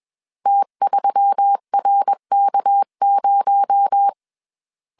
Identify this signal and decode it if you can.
Morse code, heard as a keyed tone, about 21 wpm, 800 Hz. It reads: T3RK0
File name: CW / DET / 1 6 1500C S A 20wpm 2048